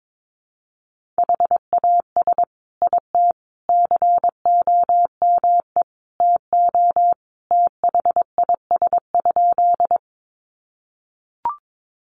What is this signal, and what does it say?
Morse code, heard as a keyed tone, about 22 wpm, 700 Hz. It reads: HAS IT COME TO THIS?
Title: Sentences from Top 100 Words 22wpm